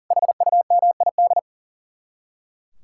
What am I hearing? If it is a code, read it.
Morse code, heard as a keyed tone, about 49 wpm, 700 Hz. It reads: HUMID
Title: Speed-Racing Common QSO Phrases Encore - 40wpm